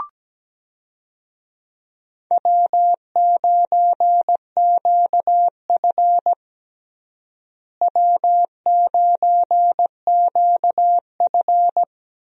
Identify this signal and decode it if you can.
Morse code, heard as a keyed tone, about 17 wpm, 700 Hz. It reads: W9QF W9QF
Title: USA Call Signs - 4 Characters - Repeated 3x - 17wpm